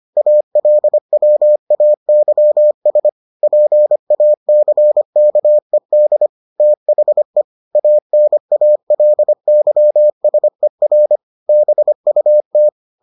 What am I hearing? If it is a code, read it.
Morse code, heard as a keyed tone, about 25 wpm, 600 Hz. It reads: ALWAYS PACKED THE ANALYSER BUT